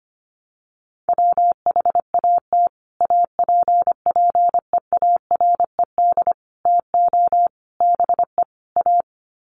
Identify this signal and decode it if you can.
Morse code, heard as a keyed tone, about 25 wpm, 700 Hz. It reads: WHAT APPEARED TO BE A
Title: Sets of 5 Words Encore2 25wpm